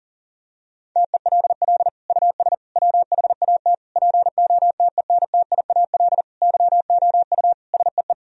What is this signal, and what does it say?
Morse code, heard as a keyed tone, about 40 wpm, 700 Hz. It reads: TELL US WHAT POTENTIAL YOU SEE
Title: Sentences from Top 1000 Words - Review Part 4 - 40wpm